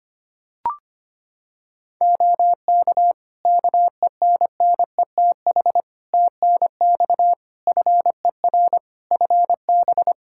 Morse code, heard as a keyed tone, about 25 wpm, 700 Hz. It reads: OK KENNETH TNX FER FB
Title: Practice QSO - Exchange 4 - Ending - 25wpm